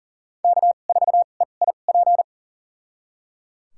Morse code, heard as a keyed tone, about 37 wpm, spaced as for 28 wpm, 700 Hz. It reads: K4EIP
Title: USA Call Signs - 2x Character Spacing - 40wpm